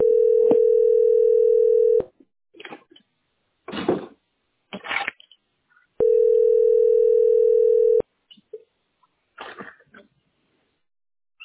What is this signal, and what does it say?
Ringback tone (North America), 440 + 480 Hz.